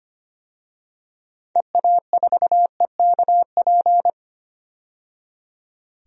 Morse code, heard as a keyed tone, about 25 wpm, 700 Hz. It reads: EA4EKP
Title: DXCC-100 Call Signs - 6 Characters - 25wpm